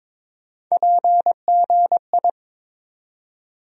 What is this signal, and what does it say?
Morse code, heard as a keyed tone, about 22 wpm, 700 Hz. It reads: PGI
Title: USA Call Sign Postfixes - 3 Letters 22wpm